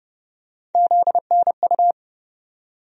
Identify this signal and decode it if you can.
Morse code, heard as a keyed tone, about 29 wpm, 700 Hz. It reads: ZNU
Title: USA Call Sign Postfixes - 3 Letters 30wpm